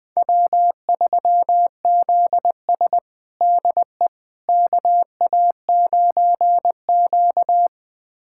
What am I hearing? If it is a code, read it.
Morse code, heard as a keyed tone, about 21 wpm, 700 Hz. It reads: W3ZS DE KA9Q
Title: USA Callsign DE Callsign - 4 Characters - 20wpm